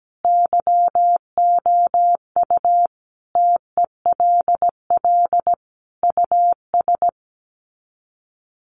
Morse code, heard as a keyed tone, about 17 wpm, 700 Hz. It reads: YOU TELL US
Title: Sets of 5 Words Encore 17wpm